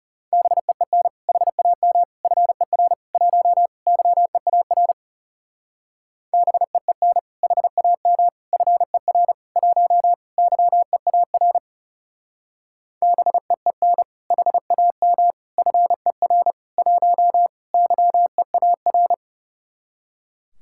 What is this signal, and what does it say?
Morse code, heard as a keyed tone, about 34 wpm, 700 Hz. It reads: BEEN HAM FER 1 YEAR BEEN HAM FER 1 YEAR BEEN HAM FER 1 YEAR
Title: Speed-Racing Common QSO Phrases - 30wpm